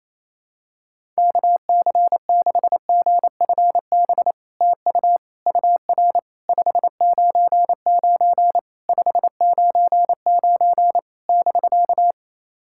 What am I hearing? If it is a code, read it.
Morse code, heard as a keyed tone, about 28 wpm, 700 Hz. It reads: KC6GFB TU UR 599 599 <BK>
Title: SOTA QSO Practice 28wpm